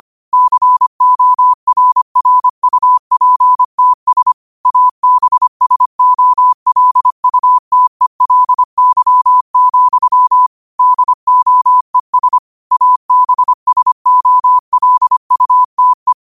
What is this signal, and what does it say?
Morse code, heard as a keyed tone, about 25 wpm, 1000 Hz. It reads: CORRUPTS ABSOLUTELY, DOES ABSOLUTE
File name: qotdfiles / 25wpm